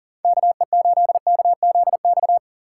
Morse code, heard as a keyed tone, about 49 wpm, 700 Hz. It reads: KE8KZX
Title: Speed Racing - USA 2x3 Call Signs 40wpm